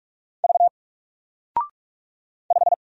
Morse code, heard as a keyed tone, about 45 wpm, 700 Hz. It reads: V 5
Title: Single Letter-Number 45wpm